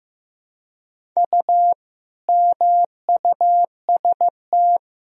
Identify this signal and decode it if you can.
Morse code, heard as a keyed tone, about 15 wpm, 700 Hz. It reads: U MUST
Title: Sentences from Top 300 Words and Abbreviations 15wpm